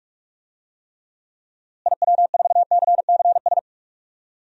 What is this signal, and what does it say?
Morse code, heard as a keyed tone, about 41 wpm, 700 Hz. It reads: IW4CCS
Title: DXCC-100 Call Signs - 6 Characters - Repeated 3x - 45wpm